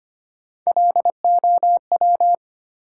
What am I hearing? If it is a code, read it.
Morse code, heard as a keyed tone, about 25 wpm, 700 Hz. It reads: LOW
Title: Three Letter Words - 25wpm